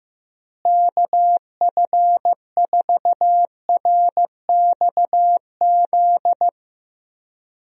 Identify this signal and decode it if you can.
Morse code, heard as a keyed tone, about 16 wpm, 700 Hz. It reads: KF4RXZ
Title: USA Call Signs - 6 Characters - Repeated 3x - 15wpm